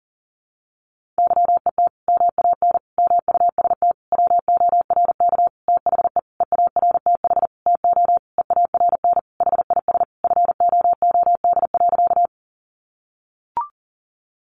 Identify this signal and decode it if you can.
Morse code, heard as a keyed tone, about 40 wpm, 700 Hz. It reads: YET MAN MUST WORK THE EARTH TO EARN HIS FOOD.
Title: Sentences from Top 1500 Words 40wpm